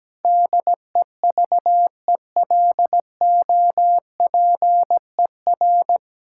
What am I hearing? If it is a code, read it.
Morse code, heard as a keyed tone, about 17 wpm, 700 Hz. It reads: DEVELOPER
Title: Practice QSO - Exchange 3 - More Info - 17wpm